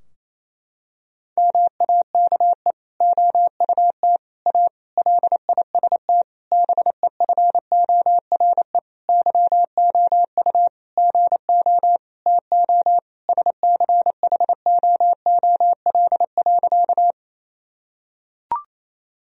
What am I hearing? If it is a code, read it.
Morse code, heard as a keyed tone, about 28 wpm, 700 Hz. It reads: MAKE OUT A LIST BEFORE YOU GO TO SCHOOL.
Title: Sentences from Top 1000 Words - Review Part 2 - 28wpm